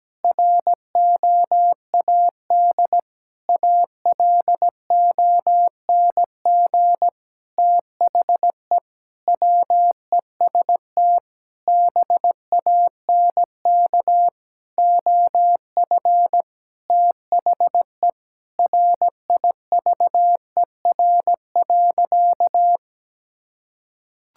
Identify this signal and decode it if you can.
Morse code, heard as a keyed tone, about 17 wpm, 700 Hz. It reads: ROAD ALONG THE WEST BANK OF THE RIVER.
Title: Sentences from Top 1000 Words - Review Part 6 - 17wpm